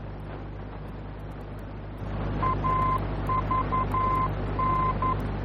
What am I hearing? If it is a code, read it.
Morse code, heard as a keyed tone, about 11 wpm, 1040 Hz, with noise. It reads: AVN
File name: france / mx / AVN-F-VOR